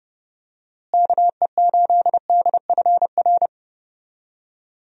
Morse code, heard as a keyed tone, about 30 wpm, 700 Hz. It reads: KE8DFR
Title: USA Call Signs - 6 Characters - Repeated 3x - 30wpm